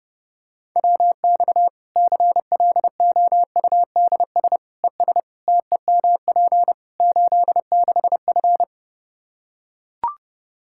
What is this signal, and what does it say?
Morse code, heard as a keyed tone, about 30 wpm, 700 Hz. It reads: WX CLOUDS ES TEMP 86F
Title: Practice QSO - Exchange 2 - Info - 30wpm